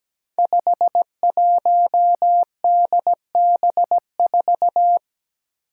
Morse code, heard as a keyed tone, about 17 wpm, 700 Hz. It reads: 51DB4
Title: Five Challenging Letter-Number Combinations 17wpm